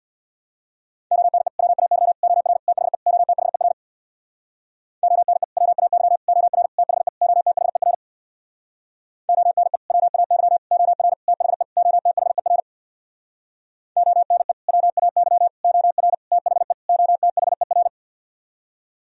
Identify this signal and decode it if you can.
Morse code, heard as a keyed tone, about 64 wpm, 700 Hz. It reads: ONE WAY OR THE OTHER ONE WAY OR THE OTHER ONE WAY OR THE OTHER ONE WAY OR THE OTHER